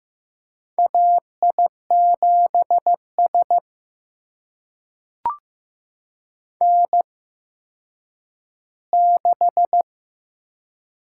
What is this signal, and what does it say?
Morse code, heard as a keyed tone, about 15 wpm, spaced as for 9 wpm, 700 Hz. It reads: AI7S N6
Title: USA Call Signs - 4 Characters - 8x Character Spacing - 15wpm